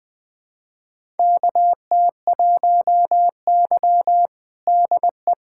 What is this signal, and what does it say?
Morse code, heard as a keyed tone, about 20 wpm, 700 Hz. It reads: KT1Y DE
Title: USA Callsign DE Callsign - 4 Characters - 20wpm